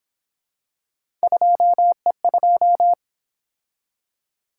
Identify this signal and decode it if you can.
Morse code, heard as a keyed tone, about 26 wpm, 700 Hz. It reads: 2E2
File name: Speed-Racing - Three Letter-Number Combinations - 22wpm